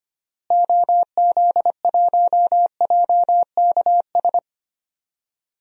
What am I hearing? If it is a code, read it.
Morse code, heard as a keyed tone, about 25 wpm, 700 Hz. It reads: OZ1JKS
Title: DXCC-100 Call Signs - 6 Characters - Repeated 2x - 25wpm